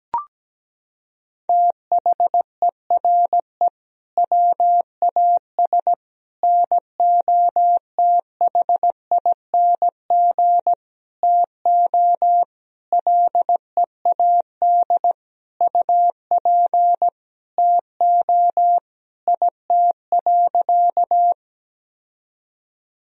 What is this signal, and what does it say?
Morse code, heard as a keyed tone, about 17 wpm, 700 Hz. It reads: THERE WAS NOTHING TO LEAD UP TO IT.
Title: Sentences from Top 1000 Words - Review Part 2 - 17wpm